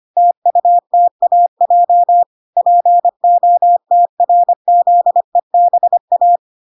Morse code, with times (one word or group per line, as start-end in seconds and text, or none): 0.2-2.2 TUTAJ
2.6-6.4 POTRZEBA